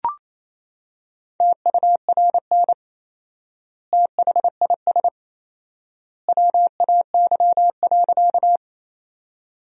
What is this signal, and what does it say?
Morse code, heard as a keyed tone, about 28 wpm, 700 Hz. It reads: TURN THIS WAY.
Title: 4x Word-Spacing - Sentences from Top 200 Words - 28wpm